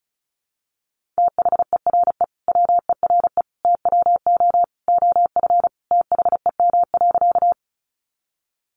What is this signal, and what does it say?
Morse code, heard as a keyed tone, about 34 wpm, 700 Hz. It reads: THERE WERE TWO OF THEM.
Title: Sentences from Top 100 Words 35wpm